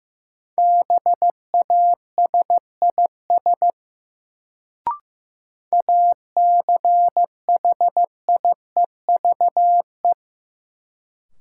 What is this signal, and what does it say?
Morse code, heard as a keyed tone, about 15 wpm, 700 Hz. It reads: BASIS ACHIEVE
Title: Sentences from Top 1000 Words - Review Part 5 - New Word Drill - 15wpm